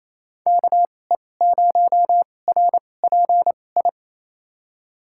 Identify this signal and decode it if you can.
Morse code, heard as a keyed tone, about 28 wpm, spaced as for 20 wpm, 700 Hz. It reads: KE0RPI
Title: USA Call Signs - 6 Characters - 2x Character Spacing - 28wpm